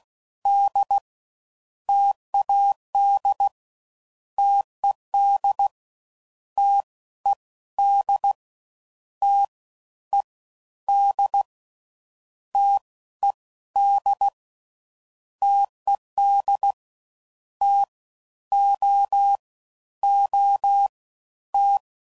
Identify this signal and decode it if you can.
Morse code, heard as a keyed tone, about 16 wpm, 800 Hz. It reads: D TAD TED T E D T E D T E D TED T O O T
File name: Combinations / Difficult Combos / T AOEI's